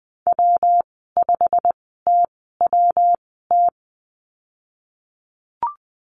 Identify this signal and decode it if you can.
Morse code, heard as a keyed tone, about 20 wpm, spaced as for 14 wpm, 700 Hz. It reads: W5TWT
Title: 186 Lesson 30g - Letter b - Call Signs 20wpm